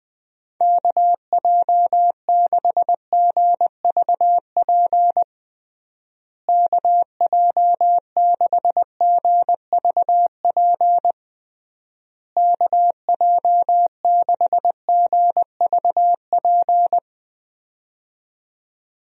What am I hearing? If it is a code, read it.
Morse code, heard as a keyed tone, about 20 wpm, 700 Hz. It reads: KJ6GVP KJ6GVP KJ6GVP
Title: USA Call Signs - 6 Characters - Repeated 3x - 20wpm